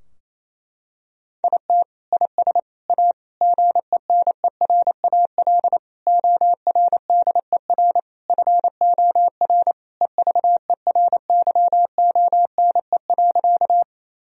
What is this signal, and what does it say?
Morse code, heard as a keyed tone, about 28 wpm, 700 Hz. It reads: IT IS A GENERAL ORDER FOR EVERYONE.
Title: Sentences from Top 1000 Words - Review Part 3 - 28wpm